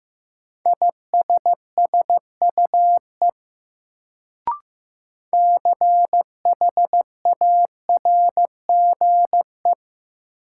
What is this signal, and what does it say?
Morse code, heard as a keyed tone, about 15 wpm, 700 Hz. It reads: ISSUE CHARGE
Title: Sentences from Top 900 Words - New Word Drill - 15wpm